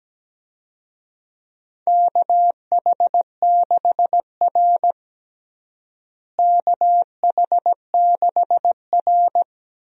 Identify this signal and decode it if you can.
Morse code, heard as a keyed tone, about 17 wpm, 700 Hz. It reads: KH6R KH6R
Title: DXCC-100 Call Signs - 4 Characters - Repeated 2x - 17wpm